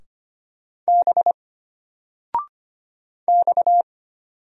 Morse code, heard as a keyed tone, about 25 wpm, 700 Hz. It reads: B X
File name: Single-Letter-Number-25wpm